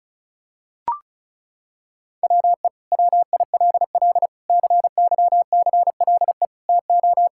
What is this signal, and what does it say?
Morse code, heard as a keyed tone, about 36 wpm, 700 Hz. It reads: WE WILL CYCLE TO
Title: Sentences from Top 1000 Words - Review Part 6 - 35wpm